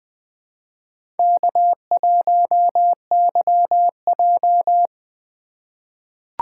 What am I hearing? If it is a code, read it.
Morse code, heard as a keyed tone, about 20 wpm, 700 Hz. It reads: K1YJ
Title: USA Call Signs - 2x Character Spacing - 20wpm